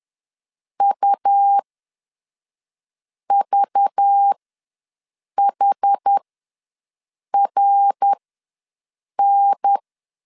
Morse code, heard as a keyed tone, about 10 wpm, spaced as for 5 wpm, 800 Hz. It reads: UVHRN
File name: LEZIONI / 5 LBVC / 5 LBV ws 10wpm 3645